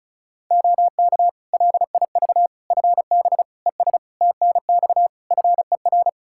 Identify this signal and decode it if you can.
Morse code, heard as a keyed tone, about 36 wpm, 700 Hz. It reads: OK LIV FB ES TNX FER
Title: Practice QSO - Exchange 2 - Info - 35wpm